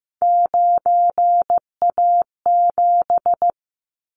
Morse code, heard as a keyed tone, about 15 wpm, 700 Hz. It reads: 9A7
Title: Callsign Prefixes for Top 100 DXCC - 15wpm